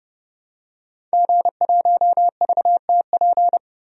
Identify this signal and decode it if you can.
Morse code, heard as a keyed tone, about 30 wpm, 690 Hz. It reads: G1VTP
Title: DXCC-100 Call Signs - 5 Characters - 30wpm